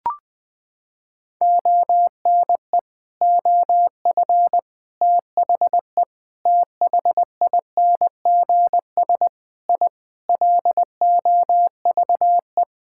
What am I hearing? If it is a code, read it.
Morse code, heard as a keyed tone, about 20 wpm, 700 Hz. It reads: ONE OF THE THINGS I LOVE
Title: Sentences with 4-Word Phrases - 20wpm